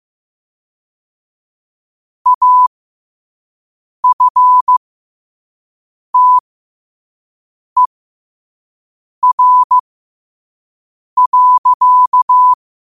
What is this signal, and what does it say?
Morse code, heard as a keyed tone, about 15 wpm, spaced as for 5 wpm, 1000 Hz. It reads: AFTER.
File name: qotdfiles / 5wpm